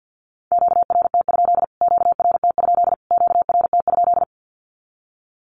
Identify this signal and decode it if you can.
Morse code, heard as a keyed tone, about 51 wpm, 700 Hz. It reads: QRT? QRT? QRT?
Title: Q-Codes Repeated 3x 50wpm